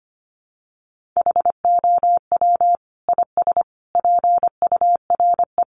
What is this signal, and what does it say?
Morse code, heard as a keyed tone, about 25 wpm, 700 Hz. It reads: HOW IS PURE
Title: Sentences from Top 1500 Words 25wpm